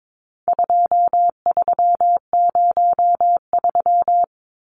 Morse code, heard as a keyed tone, about 22 wpm, 700 Hz. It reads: 2303
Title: Numbers - 4-Digit 22wpm